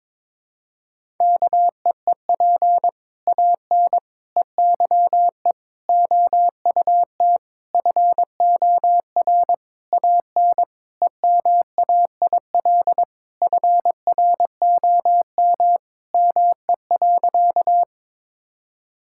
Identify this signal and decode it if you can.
Morse code, heard as a keyed tone, about 22 wpm, 700 Hz. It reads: KEEP AN EYE OUT FOR AN EMAIL FROM ME.
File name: Sentences with 7-Word Phrases - 22wpm